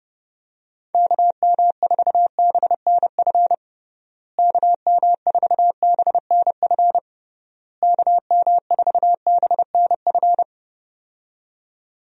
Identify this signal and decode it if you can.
Morse code, heard as a keyed tone, about 30 wpm, 700 Hz. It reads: KM4BNF KM4BNF KM4BNF
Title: USA Call Signs - 6 Characters - Repeated 3x - 30wpm